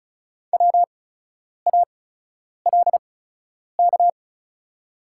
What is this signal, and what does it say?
Morse code, heard as a keyed tone, about 36 wpm, spaced as for 10 wpm, 690 Hz. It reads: WALK
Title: 8x Character Spacing - Top 300 Words - 35wpm